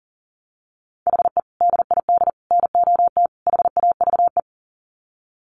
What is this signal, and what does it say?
Morse code, heard as a keyed tone, about 40 wpm, 700 Hz. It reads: HE DID NOT HAVE
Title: Sets of 4 Words 40wpm